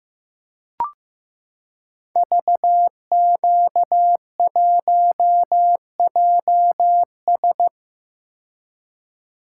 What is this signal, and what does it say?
Morse code, heard as a keyed tone, about 15 wpm, 700 Hz. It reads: VQ1JS
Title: Code Groups - 5 Letter:Numbers - 15wpm